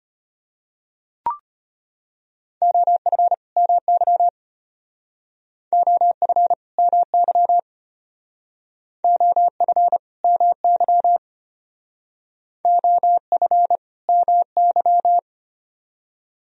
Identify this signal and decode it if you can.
Morse code, heard as a keyed tone, about 32 wpm, 700 Hz. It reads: OF MY OF MY OF MY OF MY